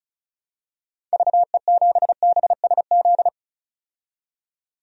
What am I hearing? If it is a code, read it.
Morse code, heard as a keyed tone, about 34 wpm, 700 Hz. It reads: VE7BSZ
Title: DXCC-100 Call Signs - 6 Characters - 35wpm